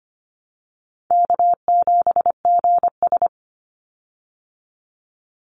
Morse code, heard as a keyed tone, about 25 wpm, 700 Hz. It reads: K7GS
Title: Call Signs Repeated 3x - 25wpm